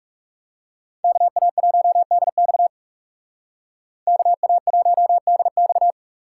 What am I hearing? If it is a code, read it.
Morse code, heard as a keyed tone, about 35 wpm, 700 Hz. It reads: KA1DX KA1DX
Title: Speed Racing - USA 2x2 Call Signs 30wpm